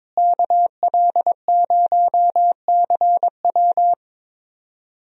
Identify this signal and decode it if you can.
Morse code, heard as a keyed tone, about 22 wpm, 700 Hz. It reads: KL0CW
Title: USA Call Signs - 5 Characters - Repeated 3x - 22wpm